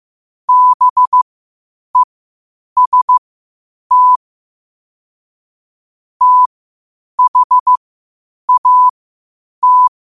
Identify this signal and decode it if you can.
Morse code, heard as a keyed tone, about 15 wpm, spaced as for 8 wpm, 1000 Hz. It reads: BEST THAT
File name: qotdfiles / 8wpm